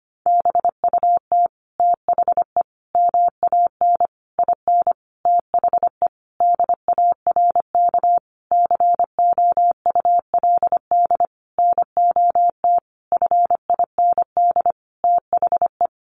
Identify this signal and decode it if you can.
Morse code, heard as a keyed tone, about 25 wpm, 700 Hz. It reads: BUT THE MAN IN THE DARK COULD NOT FIND THE
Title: Sentences from Top 600 Words 25wpm